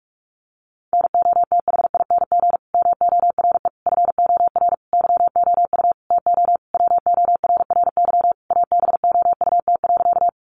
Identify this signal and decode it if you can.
Morse code, heard as a keyed tone, about 44 wpm, 700 Hz. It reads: NOTHING MORE FOR YOU TO WORRY ABOUT.
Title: Sentences from Top 1100 Words 45wpm